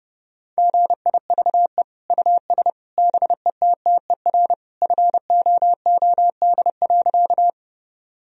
Morse code, heard as a keyed tone, about 30 wpm, 700 Hz. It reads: GIVE US BETTER FOOD.
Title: Sentences from Top 1000 Words - Review Part 1 - 30wpm